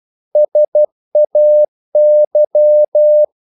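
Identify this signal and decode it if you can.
Morse code, heard as a keyed tone, about 12 wpm, 600 Hz. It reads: SAY
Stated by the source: Morse practice sending English quotations.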